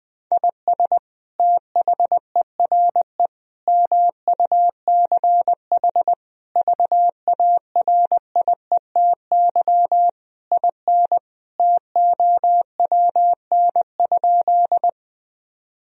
Morse code, heard as a keyed tone, about 20 wpm, 700 Hz. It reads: IS THERE MUCH VARIETY IN TOWN?
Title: Sentences from Top 1000 Words - Review Part 4 - 20wpm